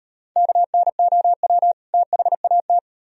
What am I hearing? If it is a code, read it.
Morse code, heard as a keyed tone, about 36 wpm, 700 Hz. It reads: KNOW THAT